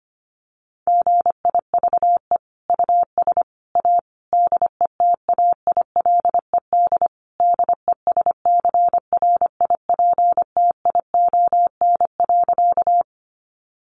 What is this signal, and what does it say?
Morse code, heard as a keyed tone, about 25 wpm, 700 Hz. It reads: GIVE US A DETAILED DESCRIPTION.
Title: Sentences from Top 1500 Words 25wpm